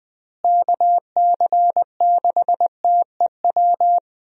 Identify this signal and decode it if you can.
Morse code, heard as a keyed tone, about 21 wpm, 700 Hz. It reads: KC6TEW
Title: USA Call Signs - 6 Characters - Repeated 3x - 20wpm